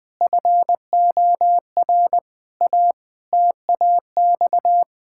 Morse code, heard as a keyed tone, about 20 wpm, 700 Hz. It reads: FOR A TAX
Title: Sentences from Top 1000 Words - Review Part 5 - 20wpm